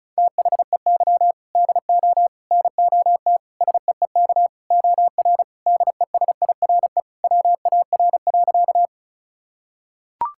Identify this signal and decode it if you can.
Morse code, heard as a keyed tone, about 34 wpm, 700 Hz. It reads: THEY DO NOT SEEK OR DESIRE WAR.
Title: Sentences from Top 1600 Words 35wpm